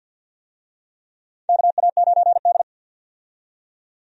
Morse code, heard as a keyed tone, about 52 wpm, 700 Hz. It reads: KA9D
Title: USA Call Signs - 4 Characters - 50wpm